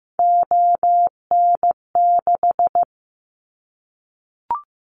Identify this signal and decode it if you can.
Morse code, heard as a keyed tone, about 15 wpm, 700 Hz. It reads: ON6
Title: Callsign Prefixes for Top 50 DXCC - 15wpm